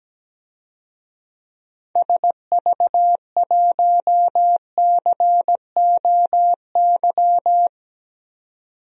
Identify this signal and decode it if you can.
Morse code, heard as a keyed tone, about 17 wpm, 700 Hz. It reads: SV1COY
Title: DXCC-100 Call Signs - 6 Characters - 17wpm